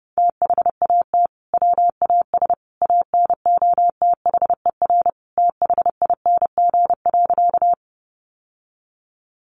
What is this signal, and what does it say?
Morse code, heard as a keyed tone, about 28 wpm, 700 Hz. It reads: THAT WAS ANOTHER THING.
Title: Sentences from Top 600 Words 30wpm